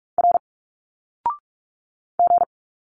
Morse code, heard as a keyed tone, about 46 wpm, 700 Hz. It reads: R G
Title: Single Letters 45wpm